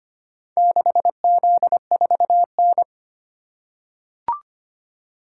Morse code, heard as a keyed tone, about 25 wpm, 700 Hz. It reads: 6Z4N
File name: practice / 25wpm / Four Letter-Number Combinations - 25wpm